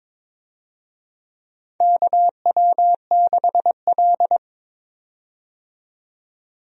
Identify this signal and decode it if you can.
Morse code, heard as a keyed tone, about 22 wpm, 700 Hz. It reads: KW6L
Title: USA Call Signs - 4 Characters - 22wpm